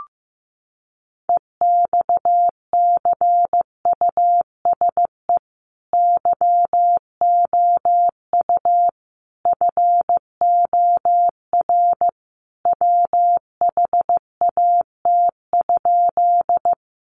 Morse code, heard as a keyed tone, about 15 wpm, 700 Hz. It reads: EXCUSE YOU FOR WHAT?
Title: Sentences from Top 700 Words 15wpm